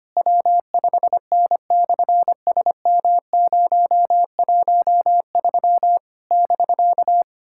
Morse code, heard as a keyed tone, about 25 wpm, 700 Hz. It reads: W5N/SM013 <BK>
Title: SOTA QSO Practice 25wpm